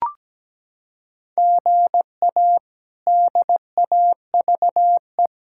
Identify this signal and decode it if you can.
Morse code, heard as a keyed tone, about 17 wpm, 700 Hz. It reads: GA DAVE
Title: K1USN SST QSO Practice - 17wpm